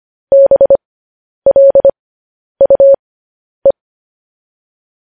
Morse code, heard as a keyed tone, about 25 wpm, spaced as for 10 wpm, 550 Hz. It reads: BLUE
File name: podcast / News headlines-1x-25x10